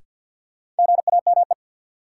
Morse code, heard as a keyed tone, about 47 wpm, 700 Hz. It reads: GAME
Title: Sentences from Top 500 Words - New Word Drill - 50wpm